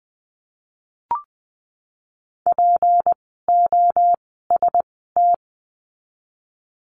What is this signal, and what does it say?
Morse code, heard as a keyed tone, about 19 wpm, spaced as for 14 wpm, 700 Hz. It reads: POST
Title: 153 Lesson 25f - Letter p - Words 20wpm